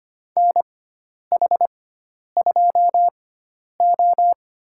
Morse code, heard as a keyed tone, about 25 wpm, spaced as for 10 wpm, 700 Hz. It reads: NH2O
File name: Lesson 16g - Letter c - Call Signs 25wpm